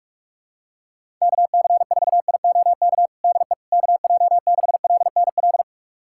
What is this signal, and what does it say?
Morse code, heard as a keyed tone, about 44 wpm, 700 Hz. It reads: KC4IOK DE KJ6LNL
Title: USA Callsign DE Callsign - 6 Characters - 45wpm